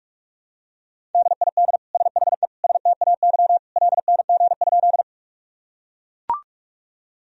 Morse code, heard as a keyed tone, about 41 wpm, 700 Hz. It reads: DID SHE STAY LNG?
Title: Sentences from Top 500 Words and Abbreviations 45wpm